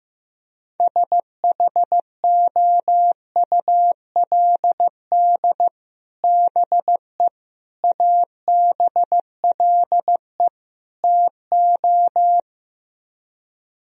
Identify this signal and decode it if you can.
Morse code, heard as a keyed tone, about 15 wpm, 700 Hz. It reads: SHOULD BE ABLE TO
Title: Speed-Racing - Sets of 4 Words 15wpm